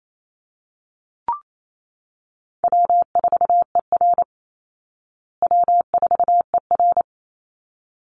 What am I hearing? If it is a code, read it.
Morse code, heard as a keyed tone, about 28 wpm, 700 Hz. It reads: W4ER W4ER
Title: Easy Call Signs Repeated 2x - 28wpm